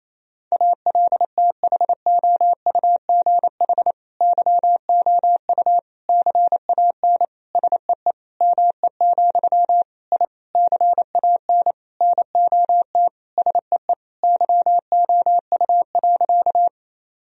Morse code, heard as a keyed tone, about 28 wpm, 700 Hz. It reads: ALTHOUGH YOU CAN SEE ME, I CAN NOT SEE YOU.
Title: Sentences from Top 1000 Words - Review Part 3 - 28wpm